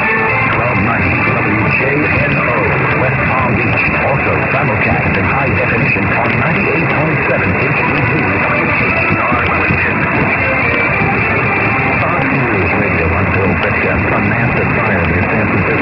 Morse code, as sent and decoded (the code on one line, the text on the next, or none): none